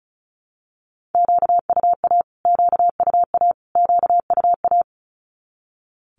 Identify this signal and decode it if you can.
Morse code, heard as a keyed tone, about 34 wpm, 700 Hz. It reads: QUA QUA QUA
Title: Q-Codes Repeated 3x 35wpm